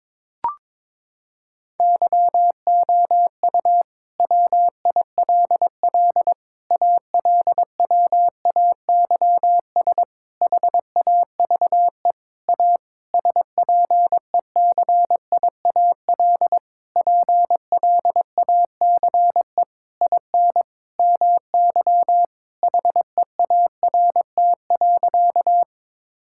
Morse code, heard as a keyed tone, about 22 wpm, 700 Hz. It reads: YOU WILL ALWAYS HAVE A SPECIAL PLACE IN MY HEART.
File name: Sentences with 9-Word Phrases - 22wpm